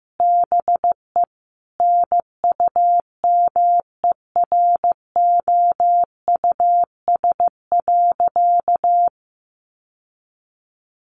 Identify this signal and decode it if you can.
Morse code, heard as a keyed tone, about 15 wpm, 700 Hz. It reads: BE NUMEROUS.
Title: Sentences from Top 1400 Words 15wpm